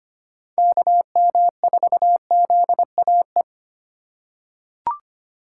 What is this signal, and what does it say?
Morse code, heard as a keyed tone, about 25 wpm, 700 Hz. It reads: KM4ZAE
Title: USA Call Signs - 6 Characters - 4x Character Spacing - 25wpm